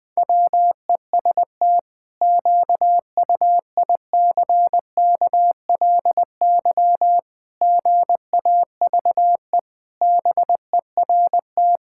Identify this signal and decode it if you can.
Morse code, heard as a keyed tone, about 20 wpm, 700 Hz. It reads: WEST QUICKLY GAVE BERT
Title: Pangram Challenge v2 - 20wpm